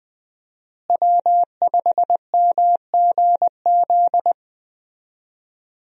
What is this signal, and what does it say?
Morse code, heard as a keyed tone, about 20 wpm, 700 Hz. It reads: W5MGZ
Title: USA Call Signs - 5 Characters - Repeated 2x - 20wpm